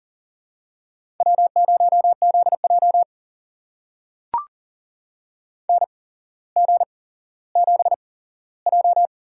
Morse code, heard as a keyed tone, about 42 wpm, spaced as for 17 wpm, 700 Hz. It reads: W0ZJ NG7J
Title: USA Call Signs - 4 Characters - 8x Character Spacing - 40wpm